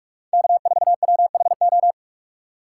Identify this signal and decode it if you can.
Morse code, heard as a keyed tone, about 42 wpm, 700 Hz. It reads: K4WHO
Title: USA Call Signs - 5 Characters - 4x Character Spacing - 45wpm